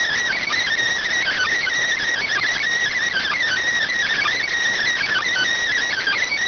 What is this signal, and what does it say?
Emergency Alert System (SAME) tones (header bursts), heard as harsh buzzing bursts. It is heard from 1.2 to 5.5 s.